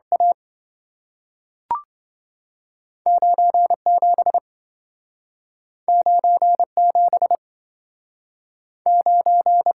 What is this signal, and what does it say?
Morse code, heard as a keyed tone, about 30 wpm, 700 Hz. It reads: A 97 97 9